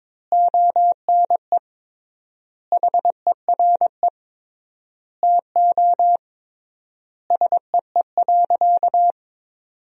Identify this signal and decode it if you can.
Morse code, heard as a keyed tone, about 22 wpm, 700 Hz. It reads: ONE HERE TO SEE.
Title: 3x Word-Spacing - Sentences from Top 200 Words - 22wpm